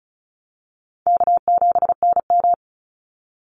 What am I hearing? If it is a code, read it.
Morse code, heard as a keyed tone, about 34 wpm, 700 Hz. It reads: K7NM
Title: Easy Call Signs Repeated 3x - 35wpm